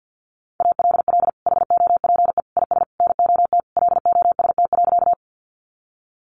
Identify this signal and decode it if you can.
Morse code, heard as a keyed tone, about 50 wpm, 700 Hz. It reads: ALL HOPE IS NOT LOST.
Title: Sentences from Top 1400 Words 50wpm